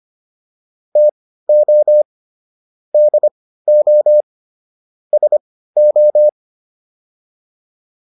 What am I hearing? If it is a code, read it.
Morse code, heard as a keyed tone, about 25 wpm, spaced as for 15 wpm, 600 Hz. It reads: TO DO SO